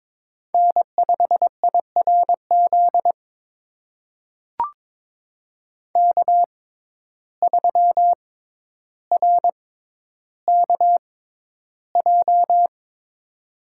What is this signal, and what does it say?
Morse code, heard as a keyed tone, about 21 wpm, spaced as for 11 wpm, 700 Hz. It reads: N5IRZ K3RKJ
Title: USA Call Signs - 5 Characters - 6x Character Spacing - 22wpm